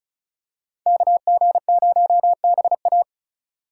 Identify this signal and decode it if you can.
Morse code, heard as a keyed tone, about 34 wpm, 700 Hz. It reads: KG0BA
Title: USA Call Signs - 2x Character Spacing - 35wpm